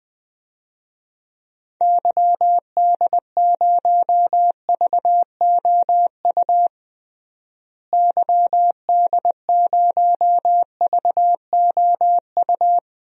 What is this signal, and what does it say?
Morse code, heard as a keyed tone, about 20 wpm, 700 Hz. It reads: YD0VOU YD0VOU
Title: DXCC-100 Call Signs - 6 Characters - Repeated 3x - 20wpm